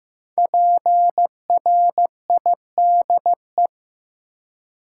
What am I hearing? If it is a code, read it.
Morse code, heard as a keyed tone, about 15 wpm, 700 Hz. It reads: PRIDE
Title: Five Letter Words - 15wpm